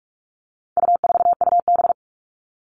Morse code, heard as a keyed tone, about 44 wpm, 700 Hz. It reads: U4UB
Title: All Cut Numbers - 4-Digit 45wpm